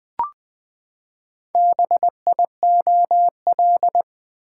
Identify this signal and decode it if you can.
Morse code, heard as a keyed tone, about 20 wpm, 700 Hz. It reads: BIOL
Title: Build A Word - 4 Letters At A Time - 10-Letter Words - 20wpm